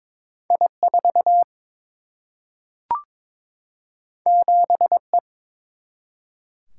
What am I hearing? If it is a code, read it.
Morse code, heard as a keyed tone, about 22 wpm, 700 Hz. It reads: I 4 7 E
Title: Single-Letter Single-Number 22wpm